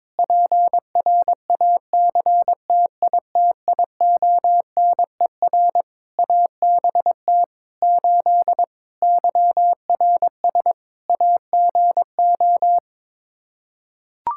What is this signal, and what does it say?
Morse code, heard as a keyed tone, about 22 wpm, 700 Hz. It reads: PRACTITIONER ABT 8 YRS AGO
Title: Practice QSO - Exchange 3 - More Info - 22wpm